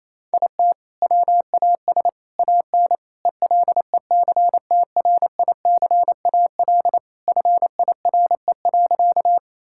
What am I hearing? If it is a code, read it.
Morse code, heard as a keyed tone, about 28 wpm, 700 Hz. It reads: IT WAS AN ELECTRICAL FIRE.
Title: Sentences from Top 1000 Words - Review Part 6 - 28wpm